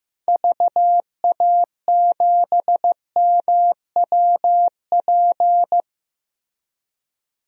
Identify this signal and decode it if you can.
Morse code, heard as a keyed tone, about 15 wpm, 700 Hz. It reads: VA7MWP
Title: DXCC-100 Call Signs - 6 Characters - Repeated 2x - 15wpm